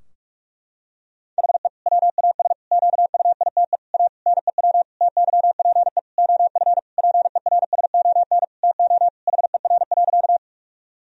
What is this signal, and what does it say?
Morse code, heard as a keyed tone, about 46 wpm, 700 Hz. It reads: HE WAS QUITE A NEW TYPE OF PERSON TO HER.
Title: Sentences from Top 1000 Words - Review Part 3 - 45wpm